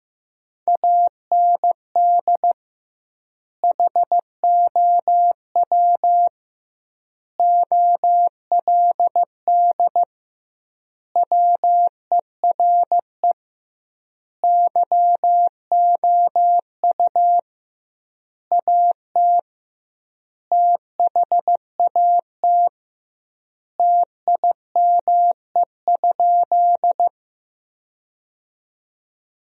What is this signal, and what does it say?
Morse code, heard as a keyed tone, about 15 wpm, 700 Hz. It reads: AND HOW OLD WERE YOU AT THAT TIME?
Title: 2x Word-Spacing - Sentences from Top 200 Words - 15wpm